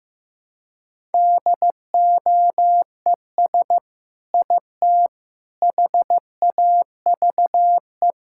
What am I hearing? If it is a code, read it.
Morse code, heard as a keyed tone, about 15 wpm, 700 Hz. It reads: DOES IT HAVE
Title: Sentences from Top 1000 Words - Review Part 5 - 15wpm